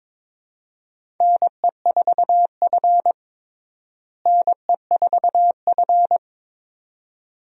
Morse code, heard as a keyed tone, about 22 wpm, 700 Hz. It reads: NE4F NE4F
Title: USA Call Signs - 4 Characters - Repeated 2x - 22wpm